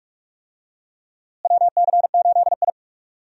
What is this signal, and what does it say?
Morse code, heard as a keyed tone, about 43 wpm, 700 Hz. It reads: WC8I
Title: USA Call Signs - 4 Characters - Repeated 2x - 45wpm